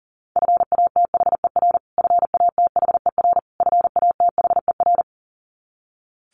Morse code, heard as a keyed tone, about 40 wpm, 700 Hz. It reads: FATHER FATHER FATHER
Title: Top 200 Words Repeated 40wpm